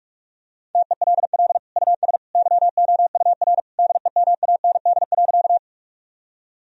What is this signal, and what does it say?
Morse code, heard as a keyed tone, about 41 wpm, 700 Hz. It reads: TELL US YOUR DEMAND.
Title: Sentences from Top 1000 Words - Review Part 4 - 45wpm